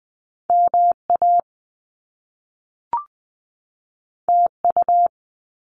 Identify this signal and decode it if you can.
Morse code, heard as a keyed tone, about 20 wpm, 700 Hz. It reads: MA TU
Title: Top Two Letter Combinations in Words 20wpm